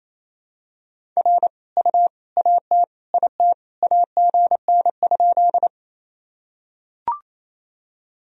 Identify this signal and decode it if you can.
Morse code, heard as a keyed tone, about 28 wpm, 700 Hz. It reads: R U AT IT AGN?
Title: Sentences from Top 200 Words and Abbreviations 28wpm